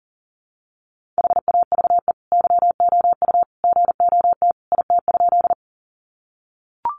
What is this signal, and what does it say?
Morse code, heard as a keyed tone, about 40 wpm, 700 Hz. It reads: HAVE YOU GOT IT?
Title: Sentences from Top 300 Words 40wpm